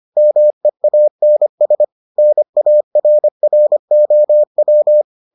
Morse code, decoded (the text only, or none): MEANS NARROW